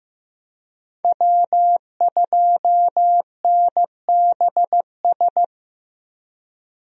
Morse code, heard as a keyed tone, about 15 wpm, 700 Hz. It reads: W2NBS
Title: USA Call Signs - 5 Characters - 15wpm